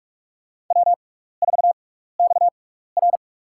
Morse code, heard as a keyed tone, about 46 wpm, spaced as for 15 wpm, 700 Hz. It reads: W4XR
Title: USA Call Signs - 4 Characters - 6x Character Spacing - 45wpm